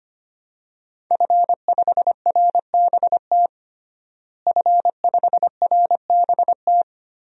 Morse code, heard as a keyed tone, about 25 wpm, 700 Hz. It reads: F5RBT F5RBT
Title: DXCC-100 Call Signs - 5 Characters - Repeated 3x - 25wpm